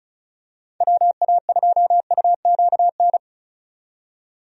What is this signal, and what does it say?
Morse code, heard as a keyed tone, about 36 wpm, 700 Hz. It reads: WA2UQN
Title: USA Call Signs - 6 Characters - 35wpm